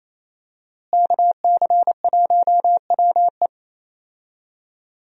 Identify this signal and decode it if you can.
Morse code, heard as a keyed tone, about 28 wpm, 700 Hz. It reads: KC1WE
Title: USA Call Signs - 5 Characters - Repeated 2x - 28wpm